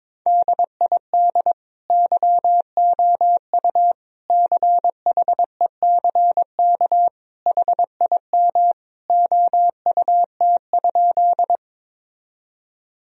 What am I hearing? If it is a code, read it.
Morse code, heard as a keyed tone, about 21 wpm, 700 Hz. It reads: DID YOU CHECK HIM OUT?
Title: Sentences from Top 1000 Words - Review Part 3 - 22wpm